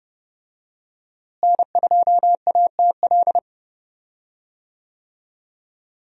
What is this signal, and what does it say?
Morse code, heard as a keyed tone, about 30 wpm, 700 Hz. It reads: N2ATL
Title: USA Call Signs - 5 Characters - Repeated 3x - 30wpm